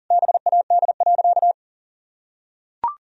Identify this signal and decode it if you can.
Morse code, heard as a keyed tone, about 40 wpm, 700 Hz. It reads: BAD.
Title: Sentences from Top 1000 Words - Review Part 2 - 40wpm